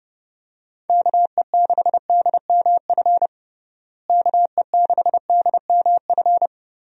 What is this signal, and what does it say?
Morse code, heard as a keyed tone, about 30 wpm, 700 Hz. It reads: KE6DMF KE6DMF
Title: USA Call Signs - 6 Characters - Repeated 3x - 30wpm